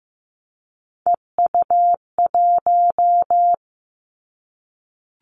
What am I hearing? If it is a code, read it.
Morse code, heard as a keyed tone, about 15 wpm, 700 Hz. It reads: EU1
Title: Callsign Prefixes for Top 50 DXCC - 15wpm